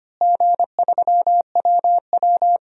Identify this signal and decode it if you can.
Morse code, heard as a keyed tone, about 25 wpm, 700 Hz. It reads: G3WW
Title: DXCC-100 Call Signs - 4 Characters - Repeated 3x - 25wpm